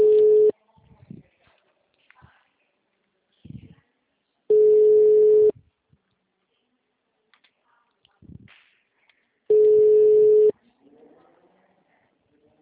Ringback tone (Europe, 425 Hz).